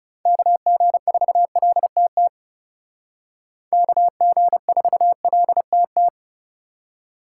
Morse code, heard as a keyed tone, about 35 wpm, 700 Hz. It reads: KG4LTT KG4LTT
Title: Speed Racing - USA 2x3 Call Signs 30wpm